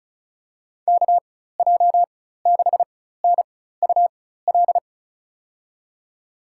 Morse code, heard as a keyed tone, about 36 wpm, spaced as for 16 wpm, 700 Hz. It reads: KJ6NUL
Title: USA Call Signs - 6 Characters - 4x Character Spacing - 35wpm